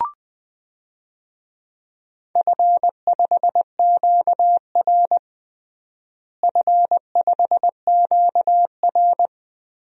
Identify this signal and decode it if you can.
Morse code, heard as a keyed tone, about 20 wpm, 700 Hz. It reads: F5QR F5QR
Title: DXCC-100 Call Signs - 4 Characters - Repeated 2x - 20wpm